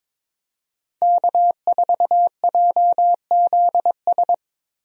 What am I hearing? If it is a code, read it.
Morse code, heard as a keyed tone, about 22 wpm, 700 Hz. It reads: K4JZS
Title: USA Call Signs - 5 Characters - 22wpm